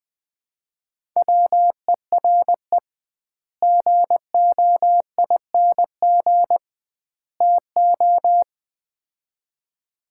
Morse code, heard as a keyed tone, about 20 wpm, 700 Hz. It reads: WERE GOING TO
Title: 2x Word-Spacing - Sets of 3 Words - 20wpm